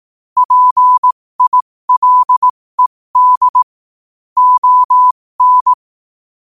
Morse code, heard as a keyed tone, about 18 wpm, spaced as for 15 wpm, 1000 Hz. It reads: PILED ON
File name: qotdfiles / 15wpm